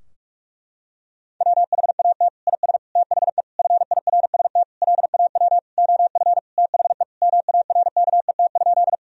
Morse code, heard as a keyed tone, about 46 wpm, 700 Hz. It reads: WHAT IS THE FIRST LAW OF THE MARKET?
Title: Sentences from Top 1000 Words - Review Part 3 - 45wpm